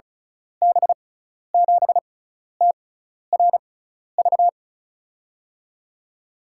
Morse code, heard as a keyed tone, about 34 wpm, spaced as for 12 wpm, 700 Hz. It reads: B7TRV
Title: USA Call Signs - 6x Character Spacing - 35wpm